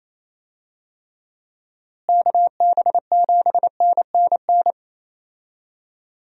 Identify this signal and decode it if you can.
Morse code, heard as a keyed tone, about 28 wpm, 700 Hz. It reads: KB7NNN
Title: USA Call Signs - 6 Characters - Repeated 2x - 28wpm